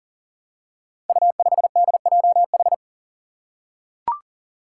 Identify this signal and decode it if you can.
Morse code, heard as a keyed tone, about 40 wpm, 700 Hz. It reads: U5DJH
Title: Five Challenging Letter-Number Combinations 40wpm